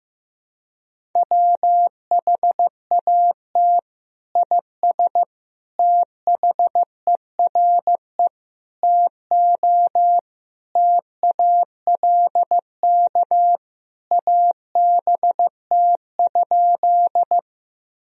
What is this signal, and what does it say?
Morse code, heard as a keyed tone, about 16 wpm, 700 Hz. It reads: WHAT IS THERE TO TALK ABT?
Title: Sentences from Top 400 Words and Abbreviations 15wpm